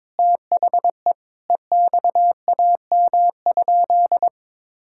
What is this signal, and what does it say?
Morse code, heard as a keyed tone, about 22 wpm, 700 Hz. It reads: THE EXAM?
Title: Sentences from Top 1000 Words - Review Part 5 - 22wpm